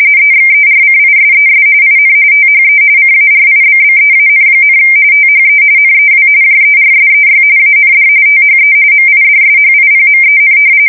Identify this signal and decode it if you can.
RTTY, about 45.45 baud. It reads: EXERCISES IN THE CARIBBEAN, THE NEW JERSEY AND THE MIDSHIPMEN
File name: fsk170 / battleshipnj